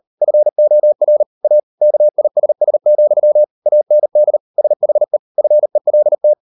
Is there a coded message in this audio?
Yes: Morse code from 0.2 to 6.3 s, heard as a keyed tone, about 38 wpm, 600 Hz. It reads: FOR A KISS, AND SHE FELT